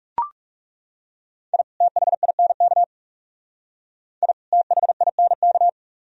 Morse code, heard as a keyed tone, about 36 wpm, 700 Hz. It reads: I THINK I THINK
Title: Speed-Racing - Sets of 2 Words 30wpm